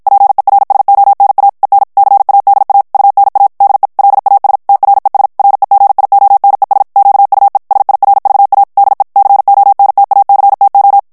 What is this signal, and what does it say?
Morse code, heard as a keyed tone, about 56 wpm, 800 Hz. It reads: PERSONA EN CADA UNA DE LAS TRES REGIONES QUE SIRVA DE CONTACTO
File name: ESPAGNOL TEXTE EN CLAIR 1 VITESSE 50 MOTS MINUTES